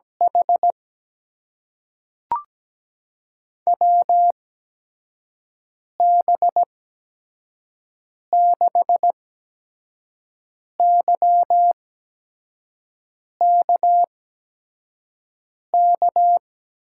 Morse code, heard as a keyed tone, about 17 wpm, spaced as for 5 wpm, 700 Hz. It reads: H WB6YKK
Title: USA Call Signs - 8x Character Spacing - 17wpm